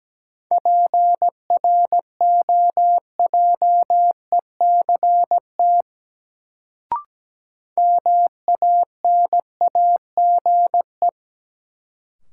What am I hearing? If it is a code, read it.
Morse code, heard as a keyed tone, about 17 wpm, 700 Hz. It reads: PROJECT MANAGE
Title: Sentences from Top 1400 Words - New Word Drill Alt - 17wpm